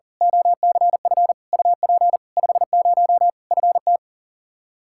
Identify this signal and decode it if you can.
Morse code, heard as a keyed tone, about 40 wpm, 700 Hz. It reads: OCF UP 50 FT
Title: Practice QSO - Exchange 2 - Info - 40wpm